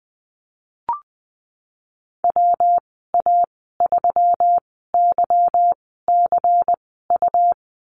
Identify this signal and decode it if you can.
Morse code, heard as a keyed tone, about 20 wpm, spaced as for 15 wpm, 700 Hz. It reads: WA3YCU
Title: 147 Lesson 24g - Letter y - Call Signs 20wpm